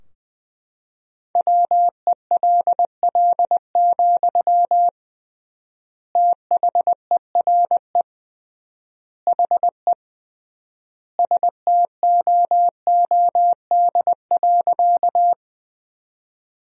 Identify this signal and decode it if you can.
Morse code, heard as a keyed tone, about 20 wpm, 700 Hz. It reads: WELL, THERE HE STOOD.
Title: 3x Word-Spacing - Sentences from Top 500 Words - 20wpm